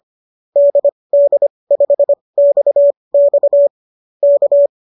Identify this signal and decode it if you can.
Morse code, heard as a keyed tone, about 25 wpm, spaced as for 20 wpm, 590 Hz. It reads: DD5XX K